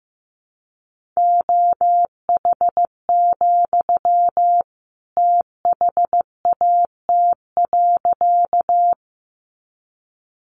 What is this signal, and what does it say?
Morse code, heard as a keyed tone, about 15 wpm, 700 Hz. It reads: OH, THAT.
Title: Sentences from Top 500 Words 15wpm